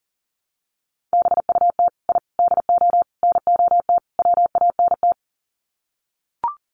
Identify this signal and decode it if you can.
Morse code, heard as a keyed tone, about 40 wpm, 700 Hz. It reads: BUT I DO NOT WANT
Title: Sets of 5 Words 40wpm